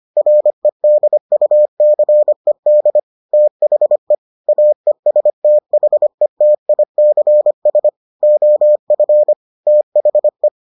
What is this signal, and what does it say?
Morse code, heard as a keyed tone, about 25 wpm, 600 Hz. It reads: REDUCED THE AESTHETICS OF THE